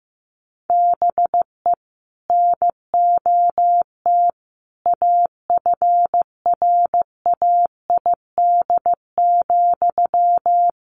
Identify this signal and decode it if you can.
Morse code, heard as a keyed tone, about 15 wpm, 700 Hz. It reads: BE NOT AFRAID,
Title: Sentences from Top 1500 Words 15wpm